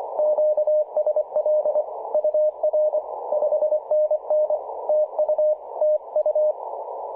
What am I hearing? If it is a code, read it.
Morse code, heard as a keyed tone, about 24 wpm, 590 Hz. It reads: QSL UR 5NN TU TU
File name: 5W0MA 18.077MHz CW